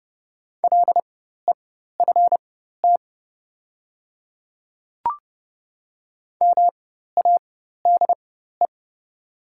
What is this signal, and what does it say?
Morse code, heard as a keyed tone, about 29 wpm, spaced as for 14 wpm, 700 Hz. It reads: LEFT MADE
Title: 4x Character Spacing - Top 300 Words - 30wpm